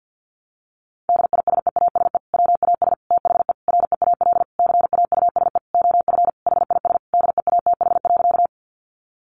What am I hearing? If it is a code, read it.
Morse code, heard as a keyed tone, about 50 wpm, 700 Hz. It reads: DISEASE WAS THE REAL CAUSE OF HIS DEATH.
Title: Sentences from Top 1100 Words 50wpm